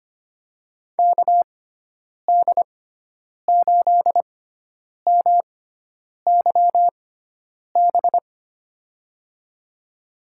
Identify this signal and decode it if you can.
Morse code, heard as a keyed tone, about 25 wpm, spaced as for 9 wpm, 700 Hz. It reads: KD8MYB
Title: USA Call Signs - 6x Character Spacing - 25wpm